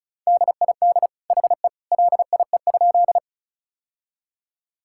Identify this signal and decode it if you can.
Morse code, heard as a keyed tone, about 36 wpm, 700 Hz. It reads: DID HE LIE?
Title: Sentences from Top 1600 Words 35wpm